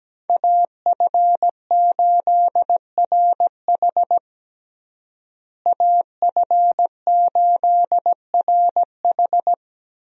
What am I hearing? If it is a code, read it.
Morse code, heard as a keyed tone, about 17 wpm, 700 Hz. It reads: AF8RH AF8RH
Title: USA Call Signs - 5 Characters - Repeated 2x - 17wpm